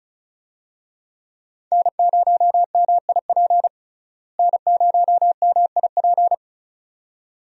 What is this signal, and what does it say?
Morse code, heard as a keyed tone, about 36 wpm, 700 Hz. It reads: N0MIP N0MIP
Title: USA Call Signs - 5 Characters - Repeated 2x - 35wpm